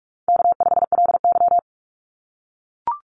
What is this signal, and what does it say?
Morse code, heard as a keyed tone, about 44 wpm, 700 Hz. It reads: K5LY
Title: Call Signs - 45wpm